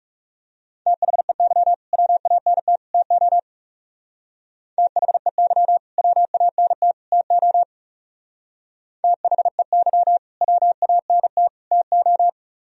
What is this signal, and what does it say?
Morse code, heard as a keyed tone, about 36 wpm, 700 Hz. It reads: THEY WANT TO THEY WANT TO THEY WANT TO
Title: Speed-Racing - Sets of 3 Words 30wpm